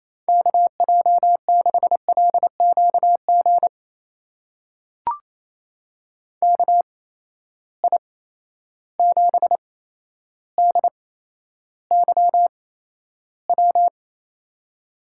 Morse code, heard as a keyed tone, about 28 wpm, spaced as for 12 wpm, 700 Hz. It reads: KJ6LQG KI7DYW
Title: USA Call Signs - 6 Characters - 8x Character Spacing - 28wpm